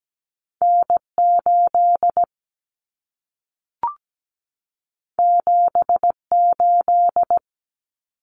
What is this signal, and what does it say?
Morse code, heard as a keyed tone, about 17 wpm, 700 Hz. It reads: N8 78
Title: Common Cut Numbers - 2-Digit 17wpm